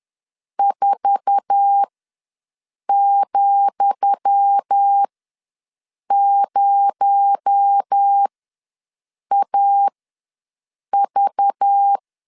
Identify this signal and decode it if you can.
Morse code, heard as a keyed tone, about 10 wpm, spaced as for 6 wpm, 800 Hz. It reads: 4,0AV
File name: LEZIONI / 10 SIGNS / 10D COMA ws 10wpm 4314